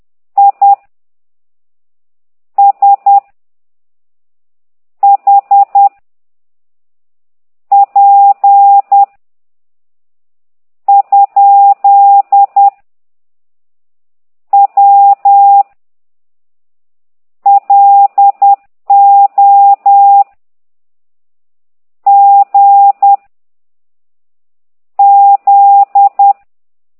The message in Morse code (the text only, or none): I S H P ? W LO G Z